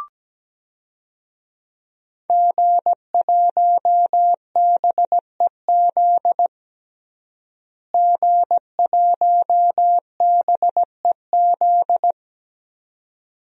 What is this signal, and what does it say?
Morse code, heard as a keyed tone, about 17 wpm, 700 Hz. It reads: G1BEZ G1BEZ
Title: DXCC-100 Call Signs - 5 Characters - Repeated 2x - 17wpm